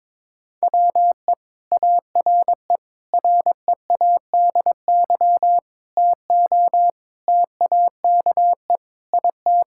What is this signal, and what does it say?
Morse code, heard as a keyed tone, about 22 wpm, 700 Hz. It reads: WE ARE READY TO TAKE IT
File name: Sentences with 7-Word Phrases - 22wpm